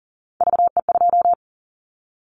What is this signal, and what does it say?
Morse code, heard as a keyed tone, about 40 wpm, 700 Hz. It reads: VE2
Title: Callsign Prefixes for Top 50 DXCC - 40wpm